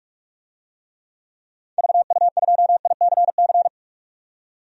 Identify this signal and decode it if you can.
Morse code, heard as a keyed tone, about 41 wpm, 700 Hz. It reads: VU2ICC
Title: DXCC-100 Call Signs - 6 Characters - 45wpm